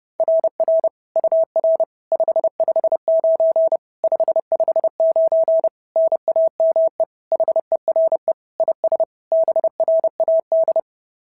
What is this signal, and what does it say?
Morse code, heard as a keyed tone, about 26 wpm, 650 Hz. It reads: RR UR 559 559 NAME HERE IS BRAD
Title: SOTA QSO Practice 30wpm